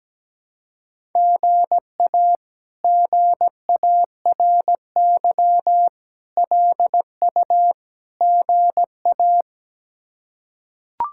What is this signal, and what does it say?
Morse code, heard as a keyed tone, about 17 wpm, 700 Hz. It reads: GA GARY LU GA
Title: K1USN SST QSO Practice - 17wpm